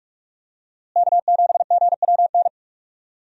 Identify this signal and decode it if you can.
Morse code, heard as a keyed tone, about 46 wpm, 700 Hz. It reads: K7GWN
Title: USA Call Signs - 5 Characters - Repeated 3x - 45wpm